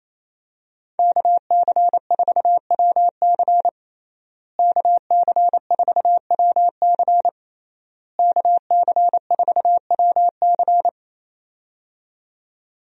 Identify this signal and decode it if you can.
Morse code, heard as a keyed tone, about 27 wpm, 700 Hz. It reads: KC4WC KC4WC KC4WC
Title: USA Call Signs - 5 Characters - Repeated 3x - 28wpm